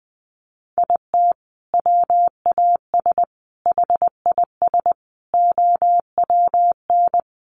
Morse code, heard as a keyed tone, about 20 wpm, 700 Hz. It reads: IT WAS HIS OWN
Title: Sentences from Top 300 Words 20wpm